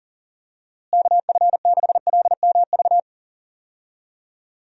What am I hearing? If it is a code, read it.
Morse code, heard as a keyed tone, about 40 wpm, 700 Hz. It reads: KF6LMV
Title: USA Call Signs - 6 Characters - Repeated 3x - 40wpm